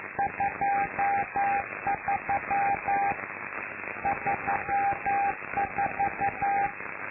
Morse code, heard as a keyed tone, about 11 wpm, 800 Hz, with noise. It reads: 23 34
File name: lu4aao / Globo Amsat